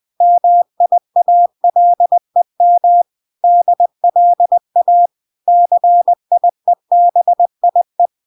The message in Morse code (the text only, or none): MIALEM DLA CIEBIE